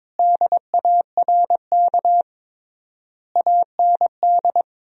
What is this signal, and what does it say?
Morse code, heard as a keyed tone, about 22 wpm, 700 Hz. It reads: DARK AND
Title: 3x Word-Spacing - Sentences from Top 500 Words - 22wpm